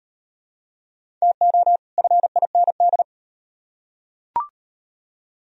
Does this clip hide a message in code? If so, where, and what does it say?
Yes: Morse code from 1.2 to 3.0 s, heard as a keyed tone, about 38 wpm, 700 Hz. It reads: TO FIND